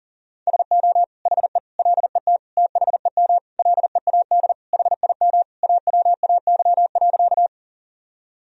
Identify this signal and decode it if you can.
Morse code, heard as a keyed tone, about 40 wpm, 700 Hz. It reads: SO HE LET THEM LEAD HIM AWAY.
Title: Sentences from Top 1000 Words - Review Part 2 - 40wpm